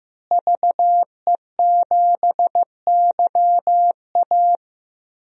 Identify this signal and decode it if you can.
Morse code, heard as a keyed tone, about 15 wpm, 700 Hz. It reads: VE7YA
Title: DXCC-100 Call Signs - 5 Characters - Repeated 2x - 15wpm